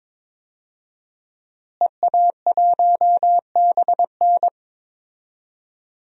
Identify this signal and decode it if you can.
Morse code, heard as a keyed tone, about 22 wpm, 700 Hz. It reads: EA1BN
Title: DXCC-100 Call Signs - 5 Characters - Repeated 2x - 22wpm